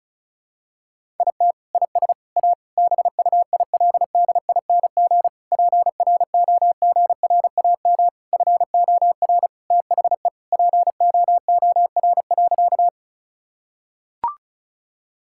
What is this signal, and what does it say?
Morse code, heard as a keyed tone, about 36 wpm, 700 Hz. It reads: IT IS A BUILDING PROGRAM FOR THE POOR.
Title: Sentences from Top 1000 Words - Review Part 5 - 35wpm